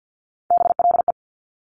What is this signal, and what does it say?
Morse code, heard as a keyed tone, about 50 wpm, 700 Hz. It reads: BLE
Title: Top Three Letter Combinations in Words 50wpm